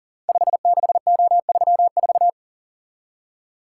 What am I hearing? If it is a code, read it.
Morse code, heard as a keyed tone, about 42 wpm, 700 Hz. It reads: 56O34
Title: Code Groups - 5 Letter:Numbers - 40wpm